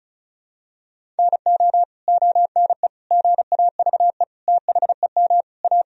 Morse code, heard as a keyed tone, about 34 wpm, 700 Hz. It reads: NO ONE GAVE THEM A
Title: Sentences from Top 1000 Words - Review Part 1 - 35wpm